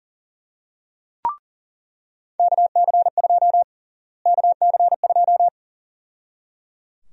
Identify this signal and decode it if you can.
Morse code, heard as a keyed tone, about 37 wpm, 700 Hz. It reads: KC2 KC2
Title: USA Call Sign Prefixes - Repeated 2x - 40wpm